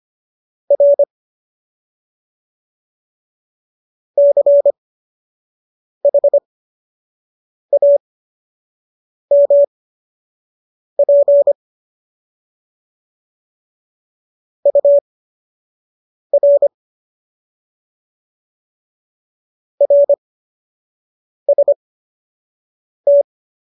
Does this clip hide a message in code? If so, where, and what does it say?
Yes: Morse code from 0.7 to 23.2 s, heard as a keyed tone, about 25 wpm, spaced as for 6 wpm, 570 Hz. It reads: R CHAMP UR RST